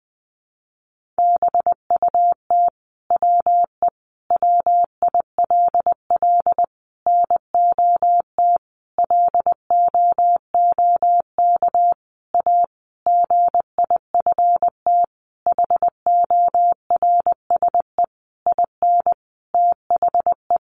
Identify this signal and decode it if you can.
Morse code, heard as a keyed tone, about 20 wpm, 700 Hz. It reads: BUT WE WILL NOT LOOK A GIFT HORSE IN THE
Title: Sentences from Top 1500 Words 20wpm